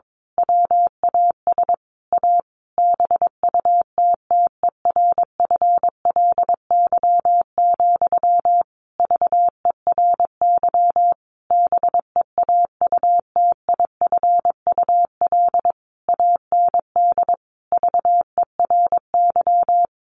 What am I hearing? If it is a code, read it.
Morse code, heard as a keyed tone, about 22 wpm, 700 Hz. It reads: WAS A BUTTERFLY, VERY BEAUTIFUL AND VERY
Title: A Sound Of Thunder 22wpm